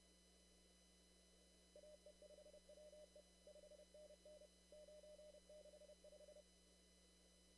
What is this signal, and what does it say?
Morse code, heard as a keyed tone, about 30 wpm, 600 Hz. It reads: AE5WE 5NN 965
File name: CW / kt5j wpx cw-op-k5pi